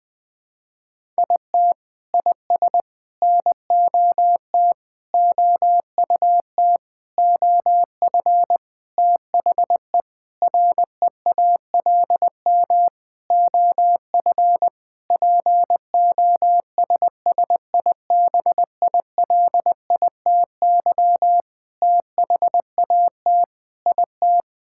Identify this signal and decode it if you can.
Morse code, heard as a keyed tone, about 20 wpm, 700 Hz. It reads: IT IS NOT OUT OF THE REALM OF POSSIBILITY THAT IT
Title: Sentences with 9-Word Phrases - 20wpm